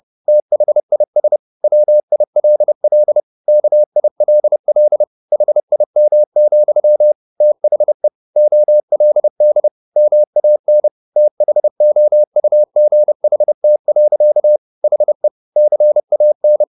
Morse code, heard as a keyed tone, about 30 wpm, 600 Hz. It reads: THIS WILL KILL HIM, THE OLD MAN THOUGHT. HE CAN
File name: the old man and the sea / the old man and the sea 1 30wpm